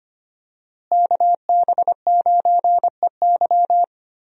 Morse code, heard as a keyed tone, about 25 wpm, 700 Hz. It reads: KB9EY
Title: USA Call Signs - 5 Characters - Repeated 2x - 25wpm